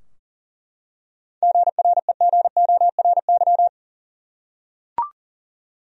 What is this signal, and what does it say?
Morse code, heard as a keyed tone, about 40 wpm, 700 Hz. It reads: GREGORY
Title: USA Common Names - 40wpm